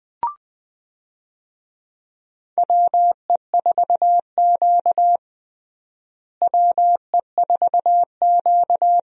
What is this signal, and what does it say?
Morse code, heard as a keyed tone, about 20 wpm, 700 Hz. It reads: WE4Q WE4Q
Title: USA Call Signs - 4 Characters - Repeated 3x - 20wpm